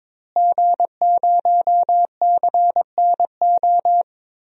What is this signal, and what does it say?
Morse code, heard as a keyed tone, about 22 wpm, 700 Hz. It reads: G0CNO
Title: DXCC-100 Call Signs - 5 Characters - Repeated 3x - 22wpm